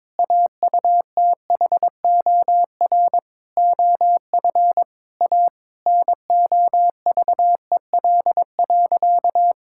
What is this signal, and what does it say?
Morse code, heard as a keyed tone, about 22 wpm, 700 Hz. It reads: AUTHOR OF A NOVEL.
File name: practice / 22wpm / Sentences with 4-Word Phrases - 22wpm